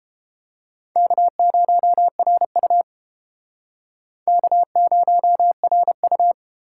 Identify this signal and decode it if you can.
Morse code, heard as a keyed tone, about 31 wpm, 700 Hz. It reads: K0RU K0RU